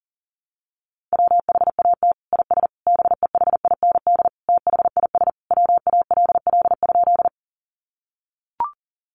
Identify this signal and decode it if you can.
Morse code, heard as a keyed tone, about 40 wpm, 700 Hz. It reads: WHAT IS BEHIND THIS WALL?
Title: Sentences from Top 1100 Words 40wpm